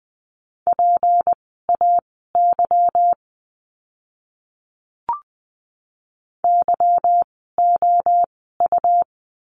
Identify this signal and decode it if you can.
Morse code, heard as a keyed tone, about 20 wpm, spaced as for 14 wpm, 700 Hz. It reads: PAY YOU
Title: 153 Lesson 25f - Letter p - Words 20wpm